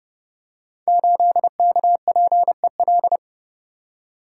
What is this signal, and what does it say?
Morse code, heard as a keyed tone, about 32 wpm, 700 Hz. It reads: 8KPEL
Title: Code Groups - 5 Letter:Numbers - 30wpm